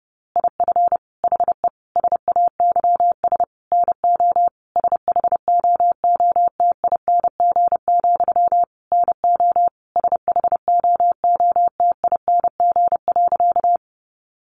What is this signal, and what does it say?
Morse code, heard as a keyed tone, about 30 wpm, 700 Hz. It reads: IF HE SAYS NO SHOOTING, NO SHOOTING.
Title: A Sound Of Thunder 30wpm